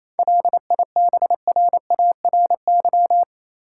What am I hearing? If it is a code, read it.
Morse code, heard as a keyed tone, about 27 wpm, 700 Hz. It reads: LIBRARY
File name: Easy Words - Speed-Racing with Hints - 22wpm